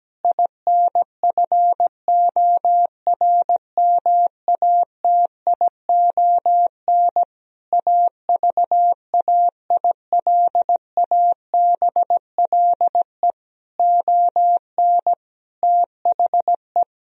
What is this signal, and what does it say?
Morse code, heard as a keyed tone, about 17 wpm, 700 Hz. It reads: INFORMATION AVAILABLE ON THE
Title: Sentences from Top 1000 Words - Review Part 3 - 17wpm